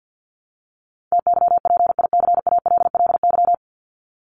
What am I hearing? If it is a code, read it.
Morse code, heard as a keyed tone, about 51 wpm, 700 Hz. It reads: TYPICALLY
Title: LY Suffix - Top 100 Words with Base Words 50wpm